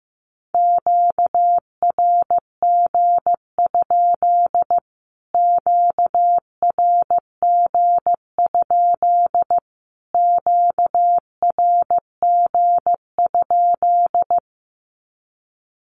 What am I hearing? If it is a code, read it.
Morse code, heard as a keyed tone, about 15 wpm, 700 Hz. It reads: QRG? QRG? QRG?
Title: Q-Codes Repeated 3x 15wpm